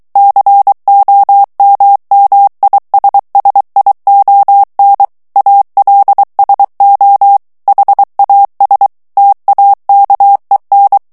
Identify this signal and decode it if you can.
Morse code, heard as a keyed tone, about 23 wpm, 800 Hz. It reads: COMMISSION ALSO HAS TAKEN
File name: ANGLAIS texte clair 3 vitesse 22 mots minute